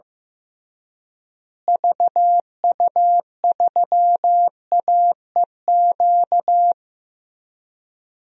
Morse code, heard as a keyed tone, about 15 wpm, 700 Hz. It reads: VU3AEQ
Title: DXCC-100 Call Signs - 6 Characters - Repeated 3x - 15wpm